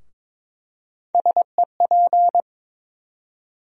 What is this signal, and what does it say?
Morse code, heard as a keyed tone, about 22 wpm, 700 Hz. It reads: SEP
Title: USA Call Sign Postfixes - 3 Letters 22wpm